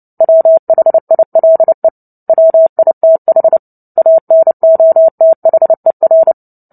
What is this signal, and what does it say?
Morse code, heard as a keyed tone, about 30 wpm, 650 Hz. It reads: WHILE WITH ANOTHER
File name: Book Two / 03-Chapter 3 - 29 WPM